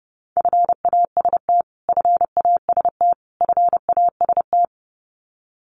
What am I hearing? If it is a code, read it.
Morse code, heard as a keyed tone, about 30 wpm, 700 Hz. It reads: FAST FAST FAST
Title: Top 500 Words Repeated 30wpm